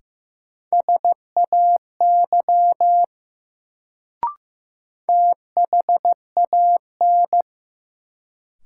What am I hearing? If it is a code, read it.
Morse code, heard as a keyed tone, about 15 wpm, 700 Hz. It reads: SAY THAN
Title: Sets of 2 Words - New Word Drill - 15wpm